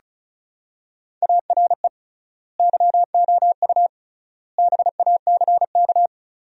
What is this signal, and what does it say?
Morse code, heard as a keyed tone, about 36 wpm, 700 Hz. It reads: ARE YOU BACK
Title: 3x Word-Spacing - Sentences from Top 200 Words - 35wpm